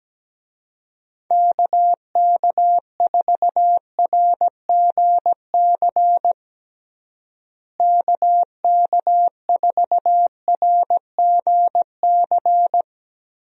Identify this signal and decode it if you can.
Morse code, heard as a keyed tone, about 17 wpm, 700 Hz. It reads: KK4RGC KK4RGC
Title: USA Call Signs - 6 Characters - Repeated 3x - 17wpm